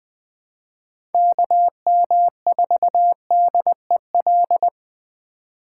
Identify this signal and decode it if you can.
Morse code, heard as a keyed tone, about 21 wpm, 700 Hz. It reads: KM4DEL
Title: Speed Racing - USA 2x3 Call Signs 17wpm